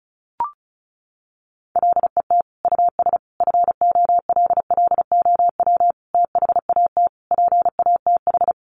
Morse code, heard as a keyed tone, about 34 wpm, 700 Hz. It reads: LET US FOLLOW THAT PATH
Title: Sentences from Top 1400 Words 35wpm